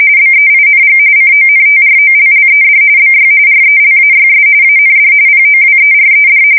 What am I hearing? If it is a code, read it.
RTTY, about 45.45 baud. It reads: BEING CONTROLLED BY RELAYS AND POWERED
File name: fsk170 / teletype-corp